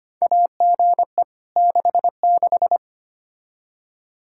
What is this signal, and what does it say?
Morse code, heard as a keyed tone, about 25 wpm, 700 Hz. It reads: AGE 66
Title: Practice QSO - Exchange 3 - More Info - 25wpm